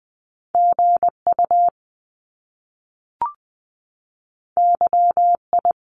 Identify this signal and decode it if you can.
Morse code, heard as a keyed tone, about 20 wpm, 700 Hz. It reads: G U Y I
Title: Top Two Letter Combinations in Words 20wpm